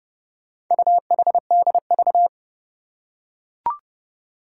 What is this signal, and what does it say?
Morse code, heard as a keyed tone, about 28 wpm, 700 Hz. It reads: UHDV
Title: Four Challenging Letter-Number Combinations 30wpm